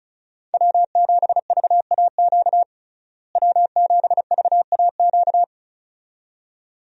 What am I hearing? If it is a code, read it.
Morse code, heard as a keyed tone, about 36 wpm, 700 Hz. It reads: W7VAQ W7VAQ
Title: USA Call Signs - 5 Characters - Repeated 2x - 35wpm